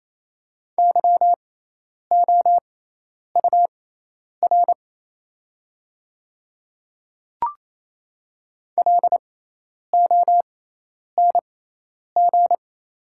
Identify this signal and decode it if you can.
Morse code, heard as a keyed tone, about 27 wpm, spaced as for 10 wpm, 700 Hz. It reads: YOUR LONG
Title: 6x Character Spacing - Top 100 Words - 28wpm